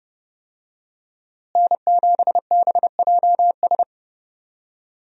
Morse code, heard as a keyed tone, about 30 wpm, 700 Hz. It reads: N7BJS
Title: USA Call Signs - 5 Characters - Repeated 2x - 30wpm